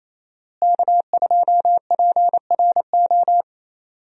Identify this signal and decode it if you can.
Morse code, heard as a keyed tone, about 28 wpm, 700 Hz. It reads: K2PRO
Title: USA Call Signs - 5 Characters - Repeated 3x - 28wpm